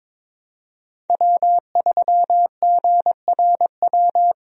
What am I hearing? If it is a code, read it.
Morse code, heard as a keyed tone, about 22 wpm, 700 Hz. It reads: W3GRW
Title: USA Call Signs - 5 Characters - Repeated 2x - 22wpm